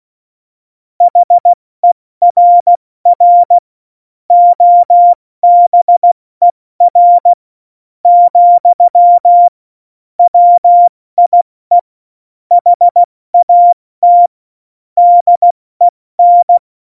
Morse code, heard as a keyed tone, about 16 wpm, 700 Hz. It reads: HERR OBER , WIE HAT DEN